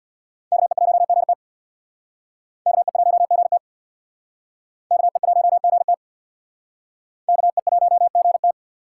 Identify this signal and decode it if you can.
Morse code, heard as a keyed tone, about 62 wpm, 700 Hz. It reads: KE1GT KE1GT KE1GT KE1GT
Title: Speed Racing - USA 2x2 Call Signs 50wpm